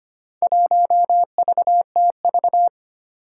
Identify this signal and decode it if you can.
Morse code, heard as a keyed tone, about 25 wpm, 700 Hz. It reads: 1VTV
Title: Four Challenging Letter-Number Combinations 25wpm